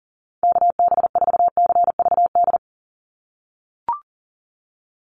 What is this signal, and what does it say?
Morse code, heard as a keyed tone, about 38 wpm, 700 Hz. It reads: KB4CVD
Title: USA Call Signs 40wpm